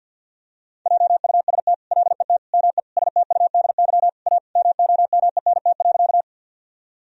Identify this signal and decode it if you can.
Morse code, heard as a keyed tone, about 50 wpm, 700 Hz. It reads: JUST LET ME STUDY A MOMENT.
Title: Sentences from Top 1000 Words - Review Part 1 - 50wpm